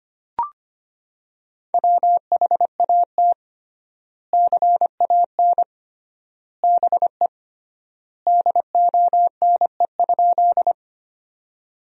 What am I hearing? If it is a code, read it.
Morse code, heard as a keyed tone, about 25 wpm, 700 Hz. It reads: WHAT CAN BE DONE?
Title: 3x Word-Spacing - Sentences from Top 500 Words - 25wpm